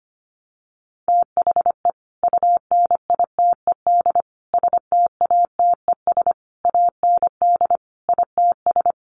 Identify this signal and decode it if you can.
Morse code, heard as a keyed tone, about 25 wpm, 700 Hz. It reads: THE UNITED STATES AND ITS
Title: Sets of 5 Words Encore 25wpm